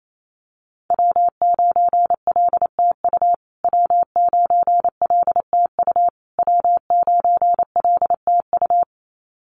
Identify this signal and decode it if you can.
Morse code, heard as a keyed tone, about 28 wpm, 700 Hz. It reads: W9LTU W9LTU W9LTU
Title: USA Call Signs Repeated 3x 28wpm